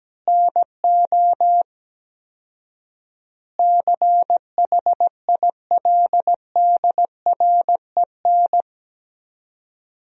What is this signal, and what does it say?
Morse code, heard as a keyed tone, about 17 wpm, 700 Hz. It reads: NO CHILDREN
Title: 4x Word-Spacing - Sentences from Top 400 Words - 17wpm